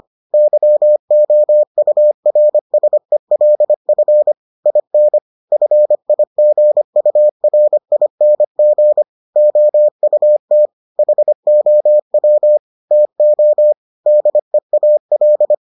Morse code, heard as a keyed tone, about 25 wpm, 600 Hz. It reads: YOURSELF IN FIGURING OUT HOW TO DEAL